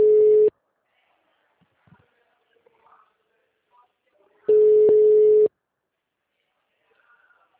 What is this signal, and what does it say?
Ringback tone (Europe, 425 Hz).